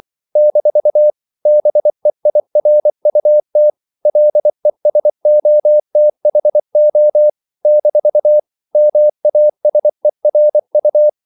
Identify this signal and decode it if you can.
Morse code, heard as a keyed tone, about 24 wpm, 600 Hz. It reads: - BEIRUT LESOTHO - MASERU
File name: CW / Practice / NationalCapitals24